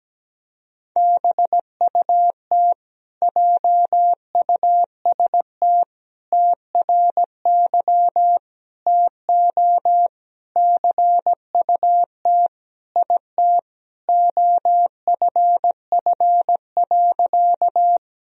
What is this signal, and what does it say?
Morse code, heard as a keyed tone, about 17 wpm, 700 Hz. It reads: BUT JUST TRY TO CUT IT OFF.
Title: Sentences from Top 1000 Words - Review Part 1 - 17wpm